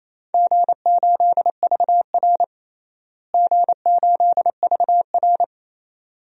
Morse code, heard as a keyed tone, about 28 wpm, 700 Hz. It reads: G8VR G8VR
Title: DXCC-100 Call Signs - 4 Characters - Repeated 2x - 28wpm